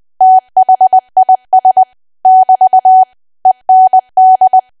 Morse code, heard as a keyed tone, about 20 wpm, 750 Hz. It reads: THIS <BT> END
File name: cwa / W1AW-Level3 / 20WPM